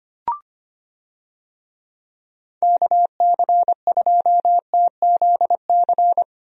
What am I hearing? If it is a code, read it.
Morse code, heard as a keyed tone, about 25 wpm, 700 Hz. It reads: KC2TZC
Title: USA Call Signs - 6 Characters - Repeated 2x - 25wpm